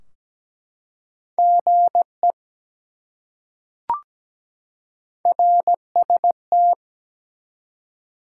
Morse code, heard as a keyed tone, about 17 wpm, 700 Hz. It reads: GE RST
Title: Top 90 QSO Elements 17wpm